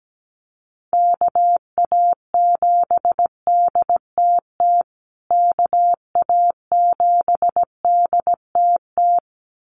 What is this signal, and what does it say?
Morse code, heard as a keyed tone, about 17 wpm, 700 Hz. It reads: KA7DTT KA7DTT
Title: USA Call Signs Repeated 2x 17wpm